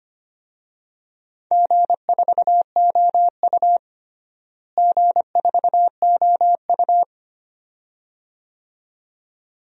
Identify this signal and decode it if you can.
Morse code, heard as a keyed tone, about 25 wpm, 700 Hz. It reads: G4OU G4OU
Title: DXCC-100 Call Signs - 4 Characters - Repeated 2x - 25wpm